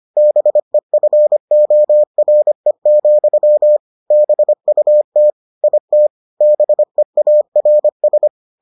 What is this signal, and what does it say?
Morse code, heard as a keyed tone, about 25 wpm, 600 Hz. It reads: BEFORE, BUT IT BEARS